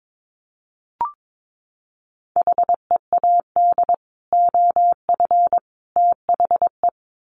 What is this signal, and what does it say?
Morse code, heard as a keyed tone, about 22 wpm, 700 Hz. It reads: HEAD OF THE
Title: Sets of 3 Words 22wpm